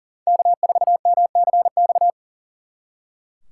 Morse code, heard as a keyed tone, about 42 wpm, 700 Hz. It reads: K4MCX
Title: Speed Racing - USA 1x3 Call Signs 40wpm